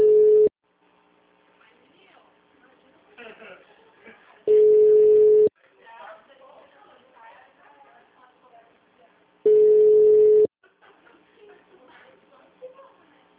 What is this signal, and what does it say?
Ringback tone (Europe, 425 Hz), with speech.